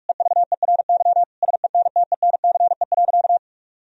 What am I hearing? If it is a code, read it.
Morse code, heard as a keyed tone, about 44 wpm, 700 Hz. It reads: EVERY SENTENCE.
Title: Sentences from Top 1000 Words - Review Part 1 - 45wpm